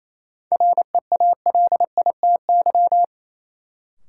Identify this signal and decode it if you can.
Morse code, heard as a keyed tone, about 28 wpm, 700 Hz. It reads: REALITY
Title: Sentences from Top 1200 Words - New Word Drill Alt - 28wpm